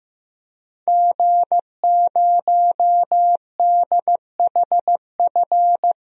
Morse code, heard as a keyed tone, about 15 wpm, 700 Hz. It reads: G0DHF
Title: DXCC-100 Call Signs - 5 Characters - Repeated 2x - 15wpm